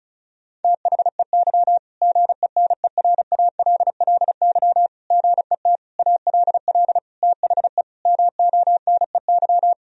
Morse code, heard as a keyed tone, about 36 wpm, 700 Hz. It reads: THEY GENERALLY GET ALL THE MONEY
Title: Sentences from Top 1000 Words - Review Part 4 - 35wpm